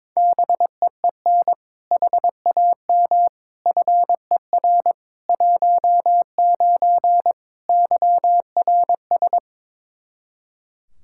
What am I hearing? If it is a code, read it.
Morse code, heard as a keyed tone, about 21 wpm, 700 Hz. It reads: BEEN HAM FER 19 YRS
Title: Practice QSO - Exchange 3 - More Info - 22wpm